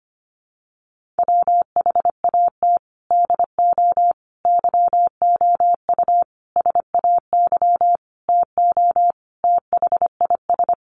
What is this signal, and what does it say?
Morse code, heard as a keyed tone, about 25 wpm, 700 Hz. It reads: WHAT DO YOU SAY TO THIS
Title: Sentences from Top 1400 Words 25wpm